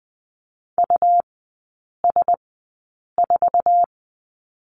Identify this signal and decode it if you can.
Morse code, heard as a keyed tone, about 20 wpm, spaced as for 8 wpm, 700 Hz. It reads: US4
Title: 138 Lesson 23e - Letter f - Three Characters 20wpm